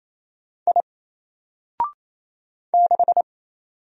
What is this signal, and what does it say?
Morse code, heard as a keyed tone, about 27 wpm, 700 Hz. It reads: I 6
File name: Single-Letter-Number-28wpm